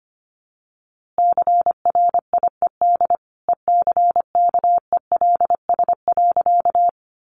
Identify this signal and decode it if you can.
Morse code, heard as a keyed tone, about 25 wpm, 700 Hz. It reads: CRIED ECKELS.
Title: A Sound Of Thunder 25wpm